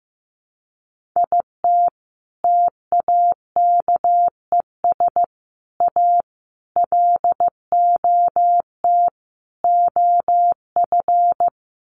Morse code, heard as a keyed tone, about 15 wpm, 700 Hz. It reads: IT TAKES A LOT OF
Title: Sets of 5 Words Encore 15wpm